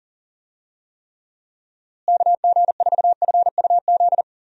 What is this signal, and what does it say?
Morse code, heard as a keyed tone, about 40 wpm, 700 Hz. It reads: KG4FUZ
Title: USA Call Signs - 6 Characters - Repeated 2x - 40wpm